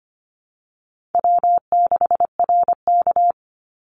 Morse code, heard as a keyed tone, about 25 wpm, 700 Hz. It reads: W6RK
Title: Easy Call Signs Repeated 3x - 25wpm